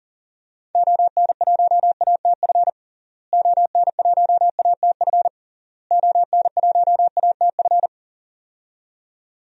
Morse code, heard as a keyed tone, about 37 wpm, 700 Hz. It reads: ON1ATF ON1ATF ON1ATF
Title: DXCC-100 Call Signs - 6 Characters - Repeated 3x - 40wpm